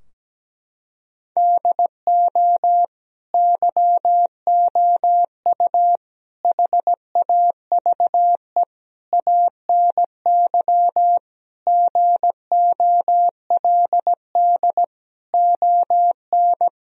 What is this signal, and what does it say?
Morse code, heard as a keyed tone, about 17 wpm, 700 Hz. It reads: DO YOU HAVE ANY GOLD ON
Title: Sentences from Top 1000 Words - Review Part 3 - 17wpm